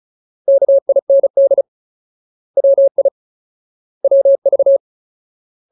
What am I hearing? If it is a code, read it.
Morse code, heard as a keyed tone, about 34 wpm, 550 Hz. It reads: KIND WI WV
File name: Ultimate Word List / 35WPM Ultimate